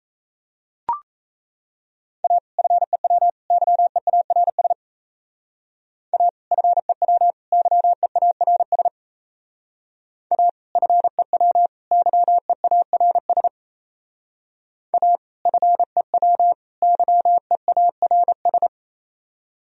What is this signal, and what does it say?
Morse code, heard as a keyed tone, about 35 wpm, 700 Hz. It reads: A FEW YEARS A FEW YEARS A FEW YEARS A FEW YEARS
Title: Speed-Racing - Sets of 3 Words 28wpm